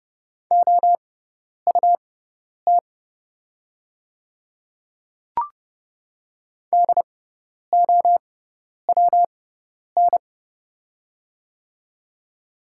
Morse code, heard as a keyed tone, about 30 wpm, spaced as for 10 wpm, 700 Hz. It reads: OUT DOWN
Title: 6x Character Spacing - Top 100 Words - 30wpm